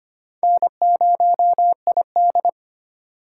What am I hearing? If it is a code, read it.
Morse code, heard as a keyed tone, about 25 wpm, 690 Hz. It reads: N0ID
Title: USA Call Signs - 4 Characters - 4x Character Spacing - 25wpm